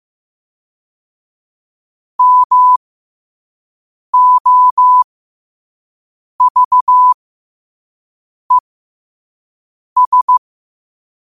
Morse code, heard as a keyed tone, about 15 wpm, spaced as for 5 wpm, 1000 Hz. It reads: MOVES
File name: qotdfiles / 5wpm